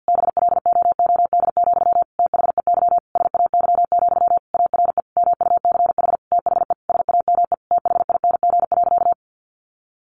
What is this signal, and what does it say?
Morse code, heard as a keyed tone, about 49 wpm, 700 Hz. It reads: BLOOD, THEY SAY, ARE MUCH THE SAME THING.
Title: Sentences from Top 1100 Words 50wpm